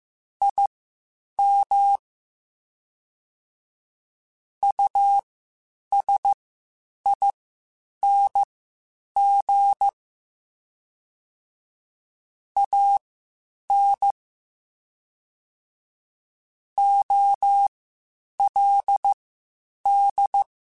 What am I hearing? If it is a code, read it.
Morse code, heard as a keyed tone, about 15 wpm, spaced as for 8 wpm, 800 Hz. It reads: IM USING AN OLD
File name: QSOs / QSOs-mp3s / WD0FBT 1@13